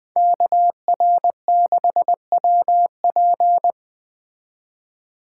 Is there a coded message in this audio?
Yes: Morse code from 0.2 to 3.7 s, heard as a keyed tone, about 20 wpm, 700 Hz. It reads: KR6WP